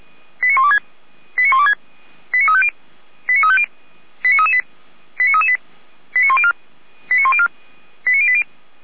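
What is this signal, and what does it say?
Five-tone selective call (ZVEI), heard as quick runs of five tones, tones of about 70 ms. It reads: 89126 89126 89350 89350 89338 89338 89104 89104 89080
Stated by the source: Five-tone ZVEI selective call received by SDR, from audio server of feuerwehr-orlamuende.org fire brigade.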